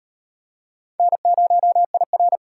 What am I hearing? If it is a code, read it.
Morse code, heard as a keyed tone, about 35 wpm, 700 Hz. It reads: N0IR